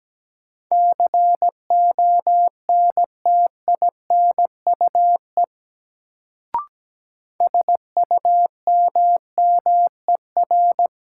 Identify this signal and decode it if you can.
Morse code, heard as a keyed tone, about 17 wpm, 700 Hz. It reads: CONTINUE SUMMER
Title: Sentences from Top 1000 Words - Review Part 6 - New Word Drill - 17wpm